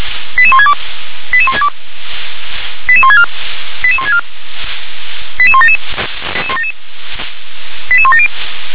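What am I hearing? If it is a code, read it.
Five-tone selective call (ZVEI), heard as quick runs of five tones, tones of about 70 ms. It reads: 88162 88162 88164 88164 88177 88170
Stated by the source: Five-tone ZVEI selective call received with an SDR.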